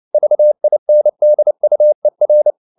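Morse code, heard as a keyed tone, about 29 wpm, 600 Hz. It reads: VINDUER
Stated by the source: Fast (QRQ) Morse at 29 wpm, from CW med Gnister.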